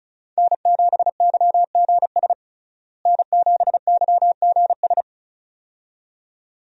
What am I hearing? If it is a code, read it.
Morse code, heard as a keyed tone, about 36 wpm, 700 Hz. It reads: N7YGS N7YGS
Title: USA Call Signs - 5 Characters - Repeated 2x - 35wpm